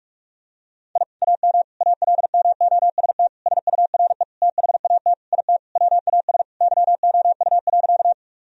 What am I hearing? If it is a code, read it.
Morse code, heard as a keyed tone, about 43 wpm, 700 Hz. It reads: I AM ALMOST SURE THAT IT WAS YOU.
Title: Sentences from Top 1000 Words - Review Part 4 - 45wpm